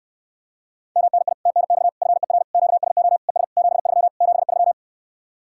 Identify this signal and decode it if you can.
Morse code, heard as a keyed tone, about 67 wpm, 700 Hz. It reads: MNI TNX FER QSO ES 73 73